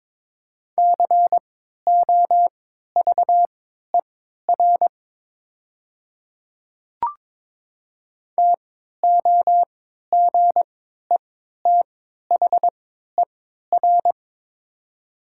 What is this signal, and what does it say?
Morse code, heard as a keyed tone, about 22 wpm, spaced as for 12 wpm, 700 Hz. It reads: COVER TOGETHER
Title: 3x Character Spacing - Top 300 Words - 22wpm